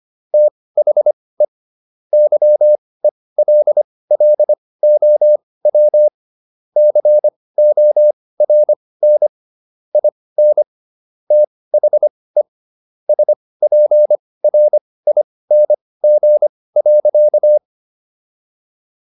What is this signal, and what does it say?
Morse code, heard as a keyed tone, about 25 wpm, spaced as for 18 wpm, 600 Hz. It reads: THE YELLOW CORN IN THE SPRING.